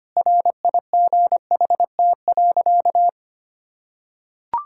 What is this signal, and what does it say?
Morse code, heard as a keyed tone, about 25 wpm, 700 Hz. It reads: RIGHT.
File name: practice / 25wpm / Sentences with 3-Word Phrases - 25wpm